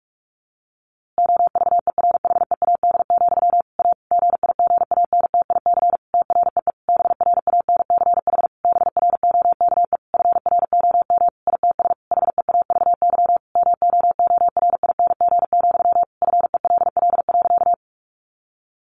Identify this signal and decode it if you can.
Morse code, heard as a keyed tone, about 44 wpm, 700 Hz. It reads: OVERHEAD, A GIGANTIC TREE BRANCH BROKE FROM ITS HEAVY MOORING, FELL.
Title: A Sound Of Thunder 45wpm